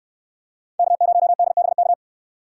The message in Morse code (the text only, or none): K0DBK